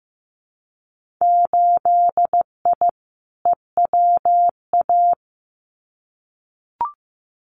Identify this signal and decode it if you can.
Morse code, heard as a keyed tone, about 15 wpm, 700 Hz. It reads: 8I EWA
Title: Winter Field Day Exchanges Encore 15wpm